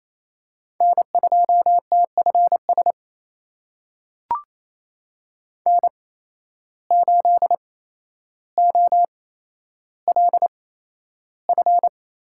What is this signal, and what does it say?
Morse code, heard as a keyed tone, about 28 wpm, spaced as for 12 wpm, 700 Hz. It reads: N2TFS N8OLF
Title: USA Call Signs - 8x Character Spacing - 28wpm